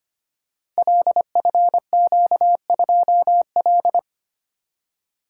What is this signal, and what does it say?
Morse code, heard as a keyed tone, about 25 wpm, 690 Hz. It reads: LFQ2L
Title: Code Groups - 5 Letter:Numbers - 25wpm